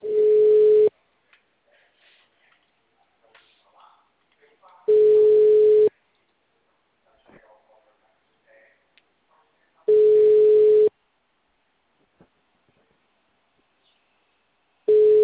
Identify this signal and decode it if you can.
Ringback tone (Europe, 425 Hz), with speech.
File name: out- -2001- - - .69